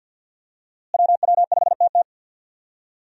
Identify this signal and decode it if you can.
Morse code, heard as a keyed tone, about 49 wpm, 700 Hz. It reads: WW5TT
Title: USA Call Signs - 5 Characters - Repeated 3x - 50wpm